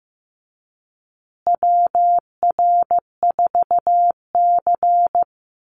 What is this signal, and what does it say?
Morse code, heard as a keyed tone, about 15 wpm, 700 Hz. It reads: WR4C
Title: USA Call Signs Repeated 3x 15wpm